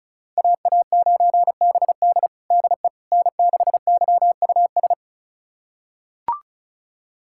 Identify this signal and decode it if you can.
Morse code, heard as a keyed tone, about 36 wpm, 700 Hz. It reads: AA9BD DE N6YUS
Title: USA Callsign DE Callsign 35wpm